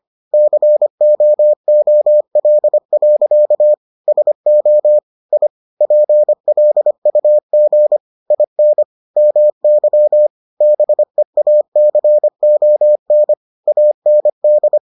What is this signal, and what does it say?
Morse code, heard as a keyed tone, about 25 wpm, 600 Hz. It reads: COOL. SO I PLUG IN MY BEACON AND